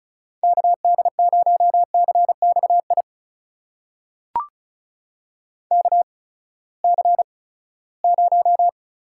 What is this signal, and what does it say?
Morse code, heard as a keyed tone, about 36 wpm, spaced as for 20 wpm, 700 Hz. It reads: KD0CXI KC0
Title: USA Call Signs - 6 Characters - 8x Character Spacing - 35wpm